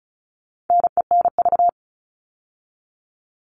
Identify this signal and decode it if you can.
Morse code, heard as a keyed tone, about 34 wpm, 700 Hz. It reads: NENV
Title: All Cut Numbers - 4-Digit 35wpm